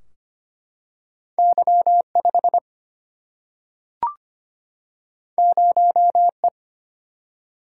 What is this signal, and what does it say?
Morse code, heard as a keyed tone, about 25 wpm, 700 Hz. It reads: Y5 0E
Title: Single-Letter Single-Number 25wpm